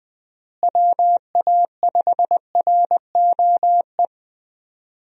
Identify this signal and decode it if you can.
Morse code, heard as a keyed tone, about 21 wpm, 700 Hz. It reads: WA5ROE
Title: USA Call Signs - 6 Characters - Repeated 3x - 20wpm